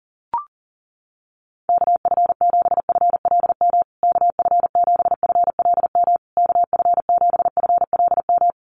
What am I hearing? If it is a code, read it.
Morse code, heard as a keyed tone, about 40 wpm, 700 Hz. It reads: KF7FLM KF7FLM KF7FLM
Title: USA Call Signs Repeated 3x 40wpm